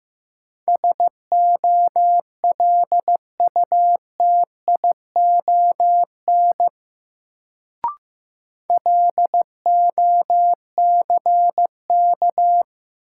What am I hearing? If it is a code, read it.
Morse code, heard as a keyed tone, about 15 wpm, 700 Hz. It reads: SOLUTION LOCK
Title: Sentences from Top 1300 Words - New Word Drill - 15wpm